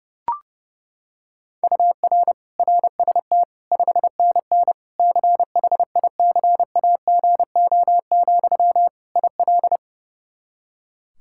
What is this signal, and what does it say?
Morse code, heard as a keyed tone, about 30 wpm, 700 Hz. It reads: UR RST 5NN CHICAGO, IL
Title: K1USN SST QSO Practice 30wpm